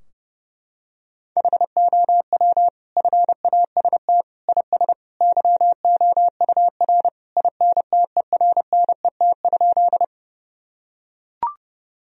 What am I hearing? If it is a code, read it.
Morse code, heard as a keyed tone, about 30 wpm, 700 Hz. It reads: HOW FAST IS YOUR INTERNET?
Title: Sentences from Top 1000 Words - Review Part 4 - 30wpm